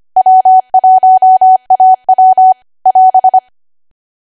Morse code, heard as a keyed tone, about 25 wpm, 750 Hz. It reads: W1AW <AS>